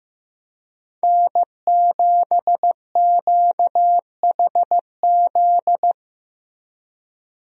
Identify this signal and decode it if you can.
Morse code, heard as a keyed tone, about 15 wpm, 700 Hz. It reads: N7QHZ
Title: USA Call Signs - 5 Characters - Repeated 2x - 15wpm